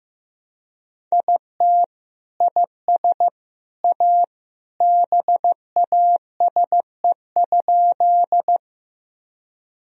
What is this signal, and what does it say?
Morse code, heard as a keyed tone, about 15 wpm, 700 Hz. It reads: IT IS A BASE?
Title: Sentences from Top 1000 Words - Review Part 1 - 15wpm